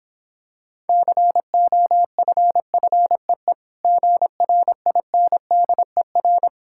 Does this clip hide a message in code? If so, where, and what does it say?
Yes: Morse code from 0.9 to 6.5 s, heard as a keyed tone, about 26 wpm, 700 Hz. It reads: COFFEE GRINDER